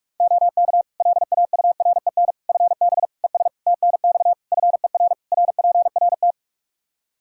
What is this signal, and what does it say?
Morse code, heard as a keyed tone, about 41 wpm, 700 Hz. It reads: OK LAUREN FB ES TNX FER RPRT
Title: Practice QSO - Exchange 2 - Info - 45wpm